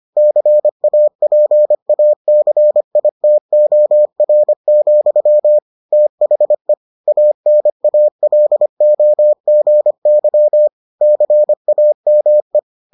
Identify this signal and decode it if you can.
Morse code, heard as a keyed tone, about 25 wpm, 600 Hz. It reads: CAPACITOR, THE ANALOGY CAME